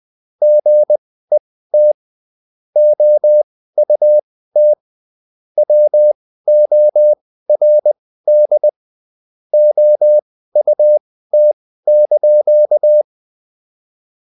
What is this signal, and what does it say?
Morse code, heard as a keyed tone, about 20 wpm, spaced as for 14 wpm, 600 Hz. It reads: GET OUT WORD OUT)